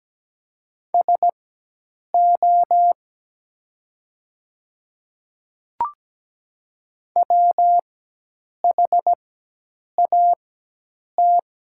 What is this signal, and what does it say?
Morse code, heard as a keyed tone, about 17 wpm, spaced as for 8 wpm, 700 Hz. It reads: SO WHAT
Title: 4x Character Spacing - Top 300 Words - 17wpm